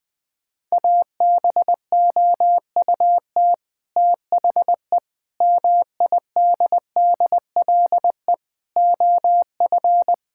Morse code, heard as a keyed tone, about 21 wpm, 700 Hz. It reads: ABOUT THE MIDDLE OF
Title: Sentences from Top 1000 Words - Review Part 6 - 20wpm